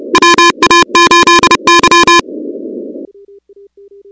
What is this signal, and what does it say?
Morse code, heard as a keyed tone, about 30 wpm, 1030 Hz, with noise. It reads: WA8Y
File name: contests / 2019-cw-wpx-kt5j-k5pi- -2330 0-16